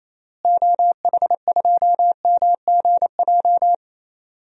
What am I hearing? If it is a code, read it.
Morse code, heard as a keyed tone, about 28 wpm, 700 Hz. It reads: OH2MGJ
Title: DXCC-100 Call Signs - 6 Characters - Repeated 3x - 28wpm